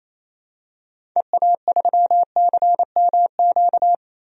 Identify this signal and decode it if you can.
Morse code, heard as a keyed tone, about 28 wpm, 700 Hz. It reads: EA3CMQ
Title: DXCC-100 Call Signs - 6 Characters - 28wpm